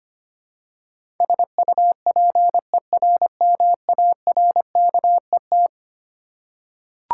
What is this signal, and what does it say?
Morse code, heard as a keyed tone, about 25 wpm, 700 Hz. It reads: SUPERMARKET
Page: morse-code-ninja-practice-25wpm